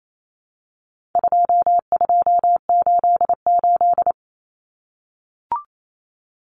Morse code, heard as a keyed tone, about 28 wpm, 700 Hz. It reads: 2288
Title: Common Cut Numbers - 4-Digit 28wpm